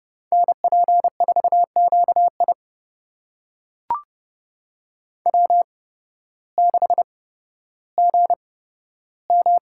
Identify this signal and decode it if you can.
Morse code, heard as a keyed tone, about 30 wpm, spaced as for 14 wpm, 700 Hz. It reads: NP4QI W6GM
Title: USA Call Signs - 5 Characters - 8x Character Spacing - 30wpm